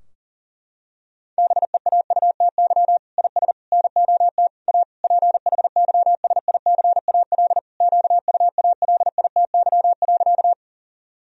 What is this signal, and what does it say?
Morse code, heard as a keyed tone, about 37 wpm, 700 Hz. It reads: BEAUTY IS NOT A PHYSICAL QUALITY.
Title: Sentences from Top 1000 Words - Review Part 4 - 40wpm